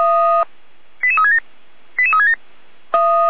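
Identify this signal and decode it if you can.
Five-tone selective call (ZVEI), heard as quick runs of five tones, tones of about 70 ms. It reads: 88367 88367
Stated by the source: ZVEI five-tone call received by SDR.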